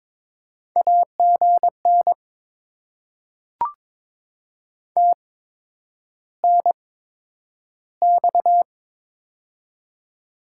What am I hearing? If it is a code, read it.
Morse code, heard as a keyed tone, about 22 wpm, spaced as for 9 wpm, 700 Hz. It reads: AGN TNX
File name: Top 20 QSO Elements - 8x Character Spacing - 22wpm